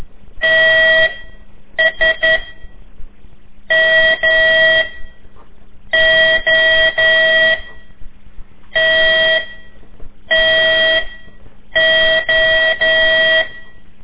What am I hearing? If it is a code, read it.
Morse code, heard as a keyed tone, about 12 wpm, 1950 Hz, with noise. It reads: T S M O T T O
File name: ik1weg2 / CORSI CW / lez4-3